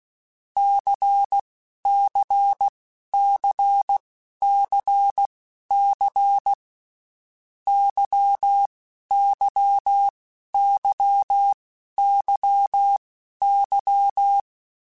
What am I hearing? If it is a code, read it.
Morse code, heard as a keyed tone, about 16 wpm, spaced as for 11 wpm, 800 Hz. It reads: CCCCC YYYYY
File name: Groups@10 / Groups@10 mp3s / LFCYQ@10